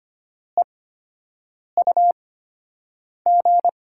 Morse code, heard as a keyed tone, about 25 wpm, spaced as for 7 wpm, 700 Hz. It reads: EUG
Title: USA Call Signs - 8x Character Spacing - 25wpm